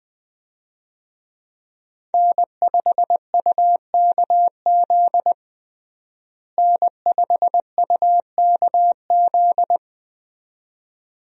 Morse code, heard as a keyed tone, about 20 wpm, 700 Hz. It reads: N5UKZ N5UKZ
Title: USA Call Signs - 5 Characters - Repeated 2x - 20wpm